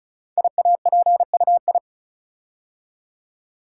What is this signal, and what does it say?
Morse code, heard as a keyed tone, about 36 wpm, 700 Hz. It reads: IAPUI
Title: Code Groups - 5 Letters - 35wpm